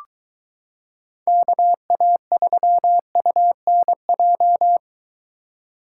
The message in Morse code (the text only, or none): KA3UNJ